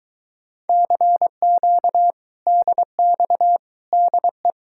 Morse code, heard as a keyed tone, about 22 wpm, 700 Hz. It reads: CQ DX DE